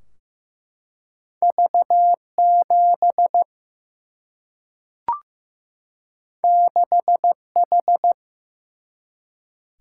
Morse code, heard as a keyed tone, about 15 wpm, 700 Hz. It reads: V7 6H
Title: Single-Letter Single-Number 15wpm